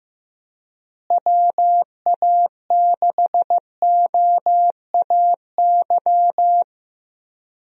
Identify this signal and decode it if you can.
Morse code, heard as a keyed tone, about 15 wpm, 700 Hz. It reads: WA6OAY
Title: USA Call Signs - 6 Characters - Repeated 2x - 15wpm